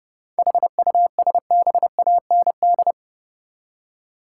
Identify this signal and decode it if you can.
Morse code, heard as a keyed tone, about 30 wpm, 700 Hz. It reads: HUSBAND
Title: Seven Letter Words - 30wpm